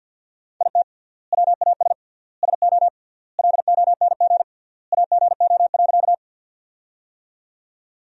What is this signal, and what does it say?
Morse code, heard as a keyed tone, about 51 wpm, 700 Hz. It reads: IT WAS SO LONG AGO.
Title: 3x Word-Spacing - Sentences from Top 500 Words - 50wpm